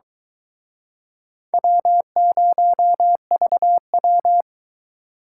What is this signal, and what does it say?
Morse code, heard as a keyed tone, about 23 wpm, 700 Hz. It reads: W0VW